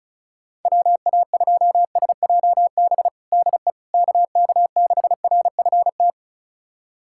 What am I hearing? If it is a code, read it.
Morse code, heard as a keyed tone, about 36 wpm, 700 Hz. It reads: WA2SJB DE KK6RFT
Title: USA Callsign DE Callsign - 6 Characters - 35wpm